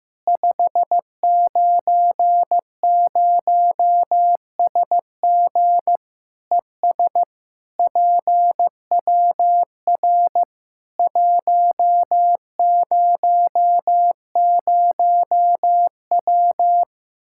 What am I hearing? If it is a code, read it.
Morse code, heard as a keyed tone, about 15 wpm, 700 Hz. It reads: 590SG ES PWR 100W
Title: K1USN SST QSO Practice 15wpm